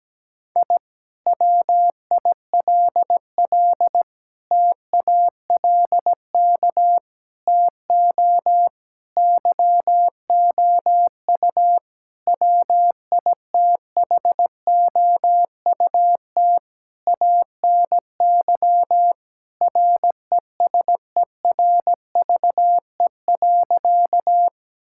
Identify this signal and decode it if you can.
Morse code, heard as a keyed tone, about 17 wpm, 700 Hz. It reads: I WILL TALK TO YOU WITHOUT ANY RESERVE.
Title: Sentences from Top 1000 Words - Review Part 6 - 17wpm